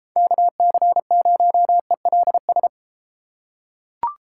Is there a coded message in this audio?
Yes: Morse code from 0.2 to 2.7 s, heard as a keyed tone, about 33 wpm, 700 Hz. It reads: KC0ELS